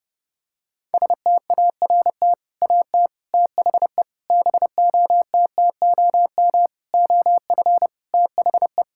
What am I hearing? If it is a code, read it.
Morse code, heard as a keyed tone, about 29 wpm, 700 Hz. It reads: START AT THE BOTTOM OF THE
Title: Sentences from Top 1000 Words - Review Part 6 - 30wpm